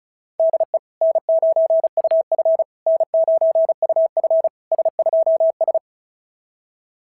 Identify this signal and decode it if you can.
Morse code, heard as a keyed tone, about 36 wpm, 690 Hz. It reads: DE N9UF N9UF S2S
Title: SOTA QSO Practice 35wpm